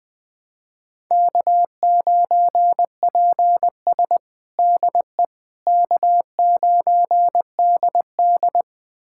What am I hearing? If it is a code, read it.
Morse code, heard as a keyed tone, about 20 wpm, 700 Hz. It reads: K9PS DE K9DD
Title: USA Callsign DE Callsign - 4 Characters - 20wpm